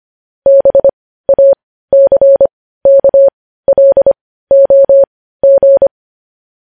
Morse code, heard as a keyed tone, about 25 wpm, spaced as for 15 wpm, 550 Hz. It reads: BACKLOG
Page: News Headlines in Morse Code at 15 WPM